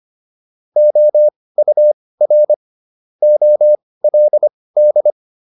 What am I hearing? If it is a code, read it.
Morse code, heard as a keyed tone, about 25 wpm, spaced as for 18 wpm, 600 Hz. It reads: OUR OLD